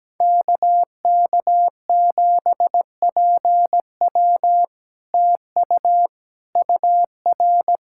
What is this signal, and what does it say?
Morse code, heard as a keyed tone, about 17 wpm, 700 Hz. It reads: KK7PW TU UR
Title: SOTA QSO Practice 17wpm